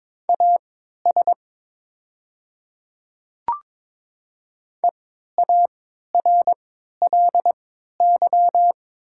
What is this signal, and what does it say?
Morse code, heard as a keyed tone, about 22 wpm, spaced as for 12 wpm, 700 Hz. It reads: AS EARLY
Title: 3x Character Spacing - Top 500 Words - 22wpm